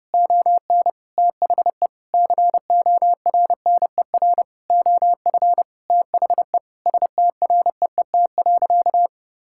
Morse code, heard as a keyed tone, about 30 wpm, 700 Hz. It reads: ON THE CORNER OF THE STREET.
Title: Sentences from Top 1600 Words 30wpm